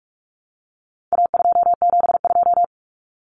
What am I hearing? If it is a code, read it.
Morse code, heard as a keyed tone, about 44 wpm, 700 Hz. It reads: A272
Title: Common Cut Numbers - 4-Digit 45wpm